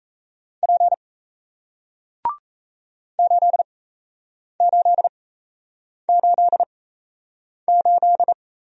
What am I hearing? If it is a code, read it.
Morse code, heard as a keyed tone, about 38 wpm, spaced as for 8 wpm, 700 Hz. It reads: P 8888